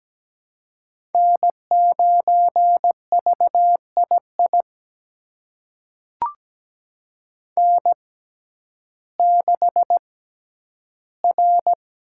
Morse code, heard as a keyed tone, about 17 wpm, spaced as for 10 wpm, 700 Hz. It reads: N9VII N6R
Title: USA Call Signs - 5 Characters - 6x Character Spacing - 17wpm